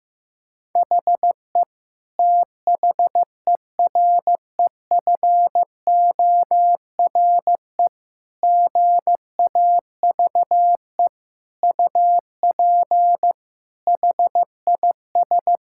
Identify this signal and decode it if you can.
Morse code, heard as a keyed tone, about 15 wpm, 700 Hz. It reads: HE THEREFORE GAVE UP HIS
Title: Sentences from Top 1000 Words - Review Part 5 - 15wpm